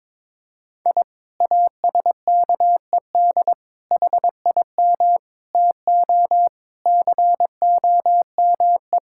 Morse code, heard as a keyed tone, about 22 wpm, 700 Hz. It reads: I ASKED HIM TO COME
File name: practice / 22wpm / Sentences with 2-Word Phrases - 22wpm